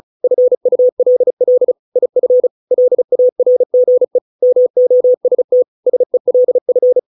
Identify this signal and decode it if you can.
Morse code, heard as a keyed tone, about 34 wpm, 500 Hz. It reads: FULL IF LARGE MOST SELF